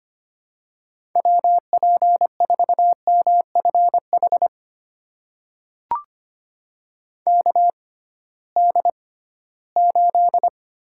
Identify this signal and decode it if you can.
Morse code, heard as a keyed tone, about 25 wpm, spaced as for 16 wpm, 700 Hz. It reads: WP4MFH KD8
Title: USA Call Signs - 6 Characters - 6x Character Spacing - 25wpm